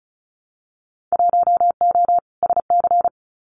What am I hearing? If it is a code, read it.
Morse code, heard as a keyed tone, about 34 wpm, 700 Hz. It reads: 1O SC
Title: Winter Field Day Exchanges 35wpm